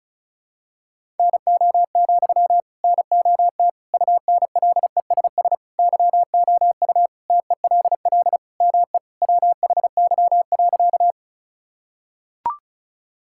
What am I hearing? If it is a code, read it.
Morse code, heard as a keyed tone, about 34 wpm, 690 Hz. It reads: NO, NOT UNLESS YOU TELL ME WHY.
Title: Sentences from Top 1000 Words - Review Part 6 - 35wpm